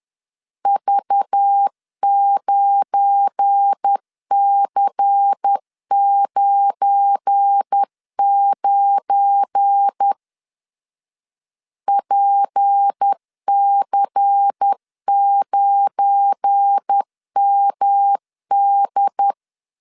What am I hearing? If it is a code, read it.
Morse code, heard as a keyed tone, about 10 wpm, 800 Hz. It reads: V9C99 PC9MD
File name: LEZIONI / 7 GF9Z / 7 GF9 10wpm 2747